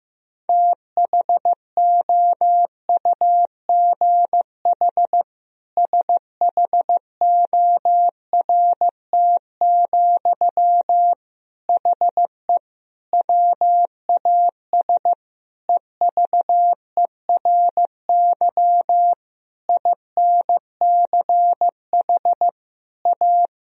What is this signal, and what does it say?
Morse code, heard as a keyed tone, about 15 wpm, 700 Hz. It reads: THOUGH SHORT, HE WAS EVERY INCH A
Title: Sentences from Top 1000 Words - Review Part 3 - 15wpm